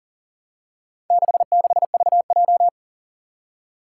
Morse code, heard as a keyed tone, about 40 wpm, 700 Hz. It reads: 66VJ
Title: Four Challenging Letter-Number Combinations 40wpm